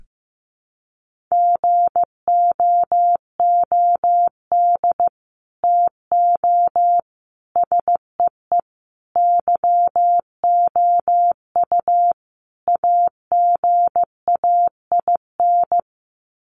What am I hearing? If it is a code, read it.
Morse code, heard as a keyed tone, about 15 wpm, 700 Hz. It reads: GOOD TO SEE YOU AGAIN
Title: Sets of 5 Words Encore2 15wpm